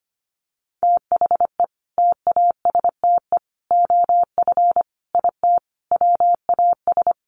Morse code, heard as a keyed tone, about 25 wpm, 700 Hz. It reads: THE TASTE OF IT WAS
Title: Sentences from Top 1100 Words 25wpm